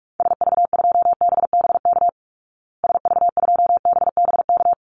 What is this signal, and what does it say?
Morse code, heard as a keyed tone, about 44 wpm, 700 Hz. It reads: SV2BBK SV2BBK
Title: Call Signs Repeated 2x - 45wpm